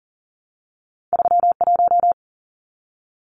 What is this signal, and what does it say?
Morse code, heard as a keyed tone, about 40 wpm, 700 Hz. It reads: 31
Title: Numbers - 2-Digit 40wpm